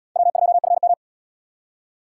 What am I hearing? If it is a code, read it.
Morse code, heard as a keyed tone, about 77 wpm, 700 Hz. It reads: W2FM